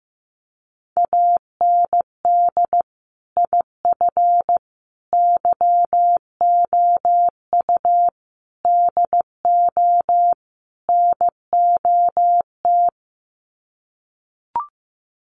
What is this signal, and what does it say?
Morse code, heard as a keyed tone, about 15 wpm, 700 Hz. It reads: AND IF YOU DO NOT
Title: Sets of 5 Words 15wpm